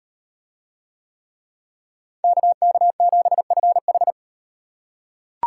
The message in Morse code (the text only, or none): KK7FH